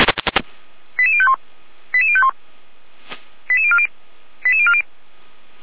Five-tone selective call (ZVEI), heard as quick runs of five tones, tones of about 70 ms. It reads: 88051 88051 88040 88040
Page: SDR/ZVEI